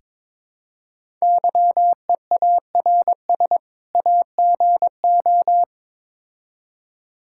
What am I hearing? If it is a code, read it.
Morse code, heard as a keyed tone, about 23 wpm, 700 Hz. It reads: YEARS AGO
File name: Build A Phrase - 2-Word Phrases - 22wpm